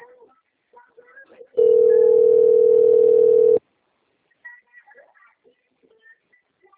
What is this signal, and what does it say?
Ringback tone (North America), 440 + 480 Hz.